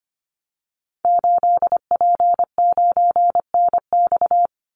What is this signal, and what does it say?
Morse code, heard as a keyed tone, about 25 wpm, 700 Hz. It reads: 8P9NX
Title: Call Signs - 25wpm